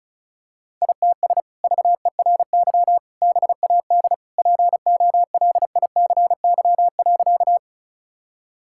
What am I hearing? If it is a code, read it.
Morse code, heard as a keyed tone, about 36 wpm, 700 Hz. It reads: ITS VERY BAD POLICY.
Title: Sentences from Top 1000 Words - Review Part 5 - 35wpm